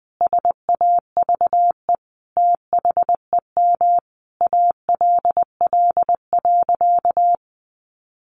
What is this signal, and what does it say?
Morse code, heard as a keyed tone, about 19 wpm, 700 Hz. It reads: SAVE THEM ALL.
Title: Sentences from Top 1100 Words 20wpm